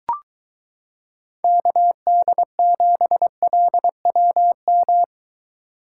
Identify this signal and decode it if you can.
Morse code, heard as a keyed tone, about 23 wpm, 700 Hz. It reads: KD7LWM